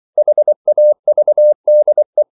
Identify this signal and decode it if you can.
Morse code, heard as a keyed tone, about 24 wpm, 600 Hz. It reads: HAVDE
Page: Ferie 24 wpm